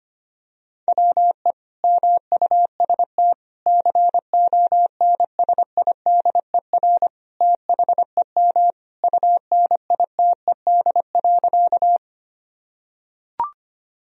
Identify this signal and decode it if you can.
Morse code, heard as a keyed tone, about 25 wpm, 700 Hz. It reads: WE MUST CONSIDER THEM UNITED.
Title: Sentences from Top 1000 Words - Review Part 4 - 25wpm